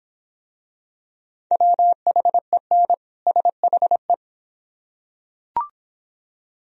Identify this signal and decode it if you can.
Morse code, heard as a keyed tone, about 23 wpm, 700 Hz. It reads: WHEN SHE